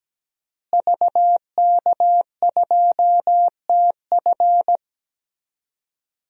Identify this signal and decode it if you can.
Morse code, heard as a keyed tone, about 17 wpm, 700 Hz. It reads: VK2TF
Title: DXCC-100 Call Signs - 5 Characters - Repeated 3x - 17wpm